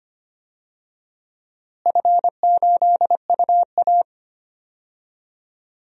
Morse code, heard as a keyed tone, about 25 wpm, 700 Hz. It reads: F8UA
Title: DXCC-100 Call Signs - 4 Characters - Repeated 3x - 25wpm